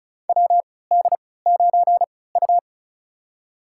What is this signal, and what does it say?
Morse code, heard as a keyed tone, about 36 wpm, spaced as for 20 wpm, 690 Hz. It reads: WD9U
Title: USA Call Signs - 4 Characters - 3x Character Spacing - 35wpm